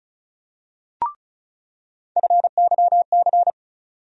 Morse code, heard as a keyed tone, about 34 wpm, 700 Hz. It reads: FYC
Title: USA Call Sign Postfixes - 3 Letters 35wpm